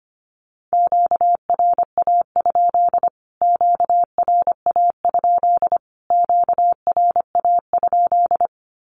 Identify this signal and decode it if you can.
Morse code, heard as a keyed tone, about 25 wpm, 700 Hz. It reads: QRA? QRA? QRA?
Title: Q-Codes Repeated 3x 25wpm